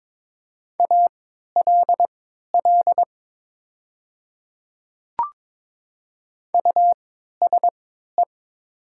Morse code, heard as a keyed tone, about 22 wpm, spaced as for 12 wpm, 700 Hz. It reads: ALL USE
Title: 3x Character Spacing - Top 100 Words - 22wpm